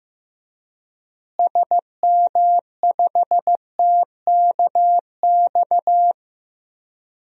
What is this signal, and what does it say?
Morse code, heard as a keyed tone, about 15 wpm, 700 Hz. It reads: SM5TKX
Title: DXCC-100 Call Signs - 6 Characters - Repeated 2x - 15wpm